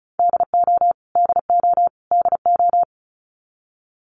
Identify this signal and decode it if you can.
Morse code, heard as a keyed tone, about 34 wpm, 700 Hz. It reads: DO DO DO
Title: Top 200 Words Repeated 35wpm